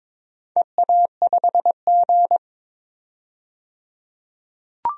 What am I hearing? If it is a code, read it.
Morse code, heard as a keyed tone, about 22 wpm, 700 Hz. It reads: EA5G
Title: DXCC-100 Call Signs - 4 Characters - 22wpm